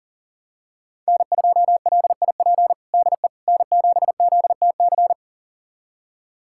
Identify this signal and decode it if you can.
Morse code, heard as a keyed tone, about 40 wpm, 700 Hz. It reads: N2LIP DE N7ZTC
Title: USA Callsign DE Callsign - 5 Characters - 40wpm